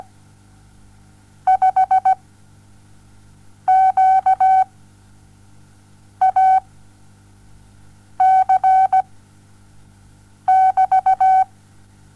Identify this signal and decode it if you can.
Morse code, heard as a keyed tone, about 17 wpm, 760 Hz. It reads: 5 Q A C <BT>
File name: Onlinelehrgang / morsen / DJ4UF Morsen07